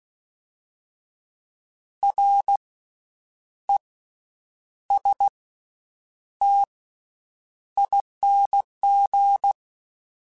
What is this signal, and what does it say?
Morse code, heard as a keyed tone, about 16 wpm, 800 Hz. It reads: R E S T ING
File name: Special INGs-mp3s / Special INGs-mp3s / Slow ing @15